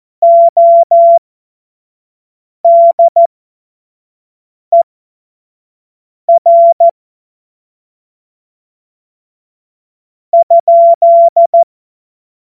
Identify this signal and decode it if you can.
Morse code, heard as a keyed tone, about 14 wpm, spaced as for 5 wpm, 680 Hz. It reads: ODER ?
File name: Morsekurs / Witze / Witz 21. September 2021, 25 F70